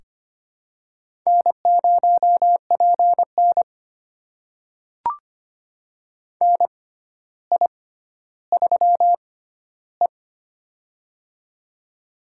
Morse code, heard as a keyed tone, about 25 wpm, spaced as for 13 wpm, 700 Hz. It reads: N0PN NI3E
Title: USA Call Signs - 4 Characters - 6x Character Spacing - 25wpm